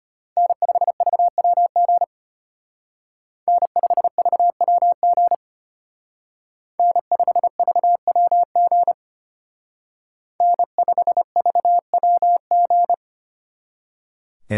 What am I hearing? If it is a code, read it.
Morse code, heard as a keyed tone, about 30 wpm, 700 Hz. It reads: N5VWG N5VWG N5VWG N5VWG
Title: Speed Racing - USA Call Signs 25wpm